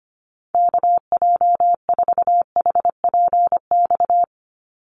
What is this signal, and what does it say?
Morse code, heard as a keyed tone, about 25 wpm, 700 Hz. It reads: KJ4HPX
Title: USA Call Signs 25wpm